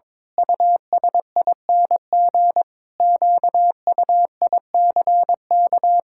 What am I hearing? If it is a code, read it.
Morse code, heard as a keyed tone, about 22 wpm, 700 Hz. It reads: USING QUICK
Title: Pangram Challenge v2 - 22wpm